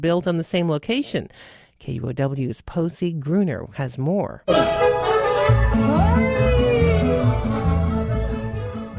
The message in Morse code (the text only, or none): none